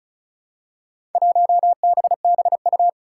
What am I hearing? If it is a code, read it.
Morse code, heard as a keyed tone, about 36 wpm, 700 Hz. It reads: 1BBU
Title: Four Challenging Letter-Number Combinations 35wpm